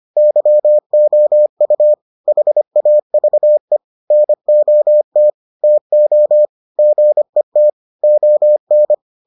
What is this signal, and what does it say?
Morse code, heard as a keyed tone, about 25 wpm, 600 Hz. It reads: YOU HAVE NOT TO GET ON